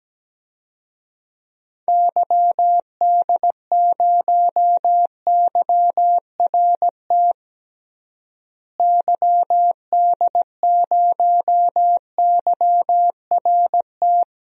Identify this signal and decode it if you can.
Morse code, heard as a keyed tone, about 17 wpm, 700 Hz. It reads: YD0YRT YD0YRT
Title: DXCC-100 Call Signs - 6 Characters - Repeated 2x - 17wpm